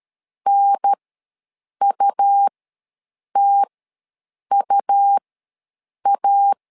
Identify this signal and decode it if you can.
Morse code, heard as a keyed tone, about 13 wpm, spaced as for 7 wpm, 800 Hz. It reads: NUTUA
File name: LEZIONI / 2 UHDN / 2 UHDN ws 12wpm 1708